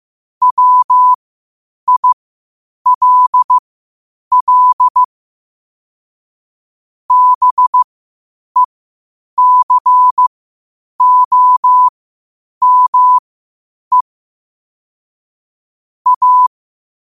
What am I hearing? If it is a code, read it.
Morse code, heard as a keyed tone, about 15 wpm, spaced as for 8 wpm, 1000 Hz. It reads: WILL BECOME A
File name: qotdfiles / 8wpm